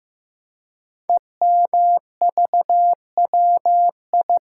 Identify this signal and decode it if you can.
Morse code, heard as a keyed tone, about 15 wpm, 700 Hz. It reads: EMVWI
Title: Code Groups - 5 Letters - 15wpm